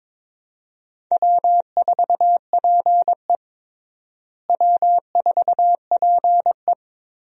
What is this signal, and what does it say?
Morse code, heard as a keyed tone, about 22 wpm, 700 Hz. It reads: W4PE W4PE
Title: USA Call Signs - 4 Characters - Repeated 3x - 22wpm